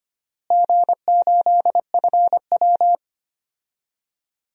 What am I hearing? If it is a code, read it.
Morse code, heard as a keyed tone, about 25 wpm, 700 Hz. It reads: G8FW
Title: DXCC-100 Call Signs - 4 Characters - Repeated 3x - 25wpm